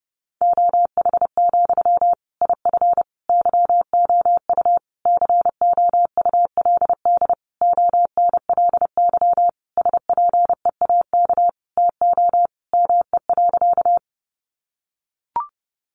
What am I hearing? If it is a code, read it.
Morse code, heard as a keyed tone, about 30 wpm, 700 Hz. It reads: OH, IF YOU COULD ONLY SPEAK TO ME.
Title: Sentences from Top 1100 Words 30wpm